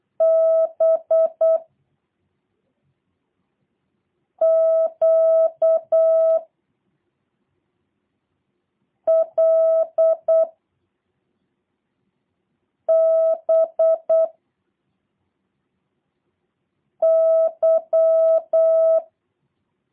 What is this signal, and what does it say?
Morse code, heard as a keyed tone, about 8 wpm, 650 Hz. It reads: B Q L B Y
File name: jh8cbh / 2023cw / review1-6